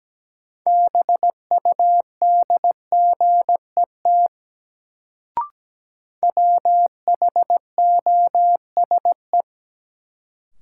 Morse code, heard as a keyed tone, about 17 wpm, 700 Hz. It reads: BUDGET WHOSE
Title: Sentences from Top 1200 Words - New Word Drill - 17wpm